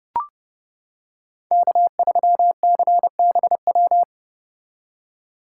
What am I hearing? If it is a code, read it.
Morse code, heard as a keyed tone, about 29 wpm, 700 Hz. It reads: K3CBW